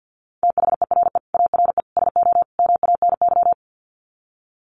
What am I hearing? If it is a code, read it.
Morse code, heard as a keyed tone, about 48 wpm, 700 Hz. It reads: THERE ARE SO MANY
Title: Sets of 4 Words 50wpm